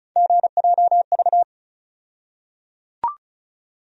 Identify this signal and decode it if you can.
Morse code, heard as a keyed tone, about 36 wpm, 700 Hz. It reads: GJV
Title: USA Call Sign Postfixes - 3 Letters 35wpm